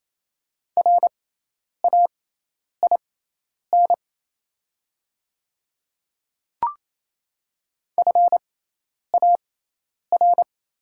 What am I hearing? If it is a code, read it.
Morse code, heard as a keyed tone, about 28 wpm, spaced as for 10 wpm, 700 Hz. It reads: RAIN FAR
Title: 6x Character Spacing - Top 500 Words - 28wpm